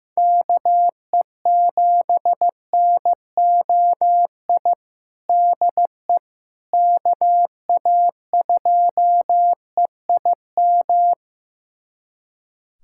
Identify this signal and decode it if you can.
Morse code, heard as a keyed tone, about 16 wpm, 700 Hz. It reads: KE7NOI DE KA2EIM
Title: USA Callsign DE Callsign - 6 Characters - 15wpm